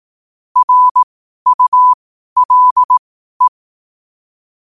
Morse code, heard as a keyed tone, about 18 wpm, spaced as for 12 wpm, 1000 Hz. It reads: RULE
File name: qotdfiles / 12wpm